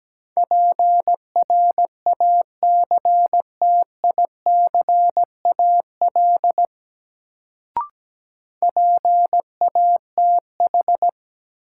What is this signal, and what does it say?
Morse code, heard as a keyed tone, about 17 wpm, 700 Hz. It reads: PRACTICAL PATH
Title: Sentences from Top 1400 Words - New Word Drill Alt - 17wpm